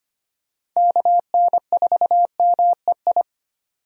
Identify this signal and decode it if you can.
Morse code, heard as a keyed tone, about 25 wpm, 700 Hz. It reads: KN4MEI
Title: USA Call Signs - 6 Characters - 25wpm